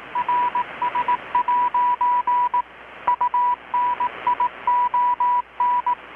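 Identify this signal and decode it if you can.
Morse code, heard as a keyed tone, about 18 wpm, 980 Hz, with radio noise. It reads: RS' UNION
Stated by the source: Morse from KPH's Night of Nights XX maritime radiotelegraphy event, received in Chiapas, Mexico about 3,800 km away on a Kenwood R-600 with a grounded 10 m long-wire antenna.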